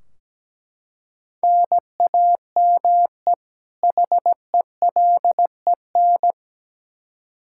Morse code, heard as a keyed tone, about 17 wpm, 700 Hz. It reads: NAME HELEN
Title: Practice QSO - Exchange 1 - Beginning - 17wpm